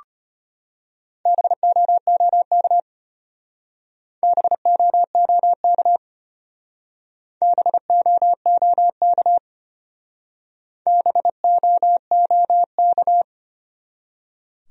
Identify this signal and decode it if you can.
Morse code, heard as a keyed tone, about 32 wpm, 700 Hz. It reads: BOOK BOOK BOOK BOOK